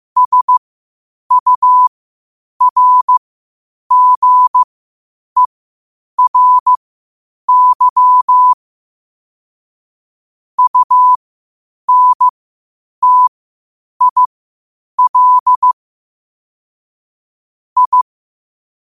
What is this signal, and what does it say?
Morse code, heard as a keyed tone, about 15 wpm, spaced as for 8 wpm, 1000 Hz. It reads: SURGERY UNTIL I
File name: qotdfiles / 8wpm